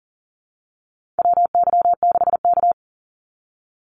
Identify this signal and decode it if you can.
Morse code, heard as a keyed tone, about 40 wpm, 700 Hz. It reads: WY6K
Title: Call Signs - 40wpm